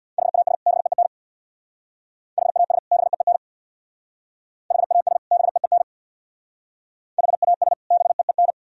Morse code, heard as a keyed tone, about 65 wpm, 700 Hz. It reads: HAS BEEN HAS BEEN HAS BEEN HAS BEEN